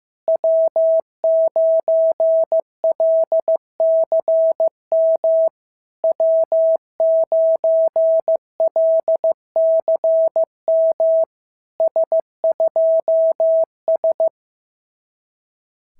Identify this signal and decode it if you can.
Morse code, heard as a keyed tone, about 15 wpm, 650 Hz. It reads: W9LCM W9LCM S2S
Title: SOTA QSO Practice 15wpm